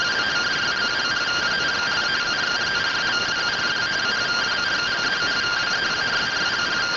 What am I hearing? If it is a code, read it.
Two-tone teleprinter signal, about 170 Hz shift.
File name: findings / CT AXIAL -485mm 2.gen-30 rtty rx